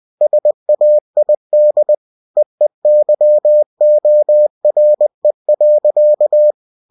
Morse code, heard as a keyed tone, about 20 wpm, 600 Hz. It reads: SAID EEYORE.